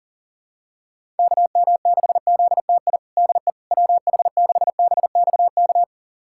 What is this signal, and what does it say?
Morse code, heard as a keyed tone, about 40 wpm, 700 Hz. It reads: KM6ZTI DE WH6BXK
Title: USA Callsign DE Callsign - 6 Characters - 40wpm